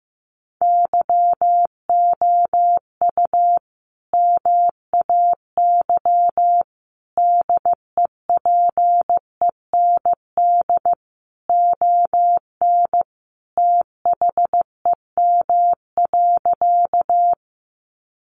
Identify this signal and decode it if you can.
Morse code, heard as a keyed tone, about 15 wpm, 700 Hz. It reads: YOU MAY DEPEND ON THEM.
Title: Sentences from Top 1300 Words 15wpm